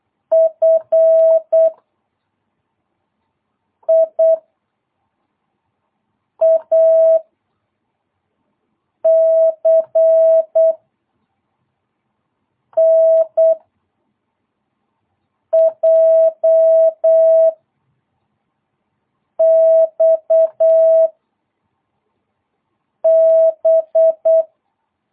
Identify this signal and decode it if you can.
Morse code, heard as a keyed tone, about 8 wpm, 660 Hz. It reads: F I A C N J X B
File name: jh8cbh / 2022cw